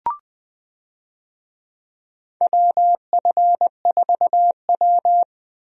Morse code, heard as a keyed tone, about 20 wpm, 700 Hz. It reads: WF4W
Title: USA Call Signs - 4 Characters - Repeated 2x - 20wpm